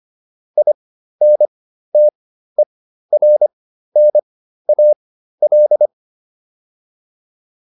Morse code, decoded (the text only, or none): INTERNAL